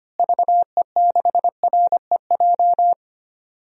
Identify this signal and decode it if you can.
Morse code, heard as a keyed tone, about 25 wpm, 700 Hz. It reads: VE6REJ
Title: DXCC-100 Call Signs - 6 Characters - Repeated 3x - 25wpm